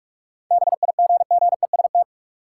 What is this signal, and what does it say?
Morse code, heard as a keyed tone, about 46 wpm, 700 Hz. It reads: BIGGEST
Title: Sets of 4 Words - New Word Drill - 45wpm